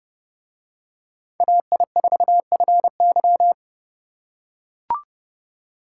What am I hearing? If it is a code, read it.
Morse code, heard as a keyed tone, about 29 wpm, 700 Hz. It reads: AI4FY